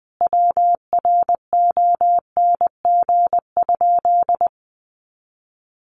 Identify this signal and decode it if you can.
Morse code, heard as a keyed tone, about 20 wpm, 700 Hz. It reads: WRONG?
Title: Sentences from Top 900 Words 20wpm